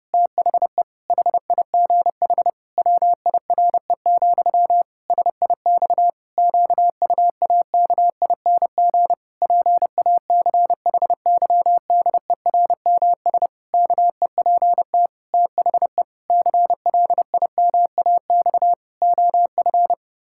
Morse code, heard as a keyed tone, about 30 wpm, 700 Hz. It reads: THE HIGH WIRE, SIX QUAKING PACHYDERMS KEPT THE CLIMAX OF
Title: Pangram Challenge v2 - 30wpm